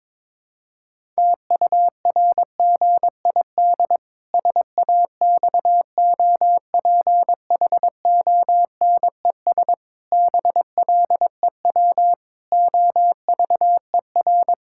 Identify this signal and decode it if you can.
Morse code, heard as a keyed tone, about 22 wpm, 700 Hz. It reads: TURGID SAXOPHONES BLEW OVER
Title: Pangram Challenge v2 - 22wpm